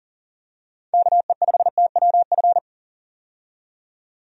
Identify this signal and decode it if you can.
Morse code, heard as a keyed tone, about 40 wpm, 700 Hz. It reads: KE5TWF
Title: USA Call Signs - 6 Characters - 40wpm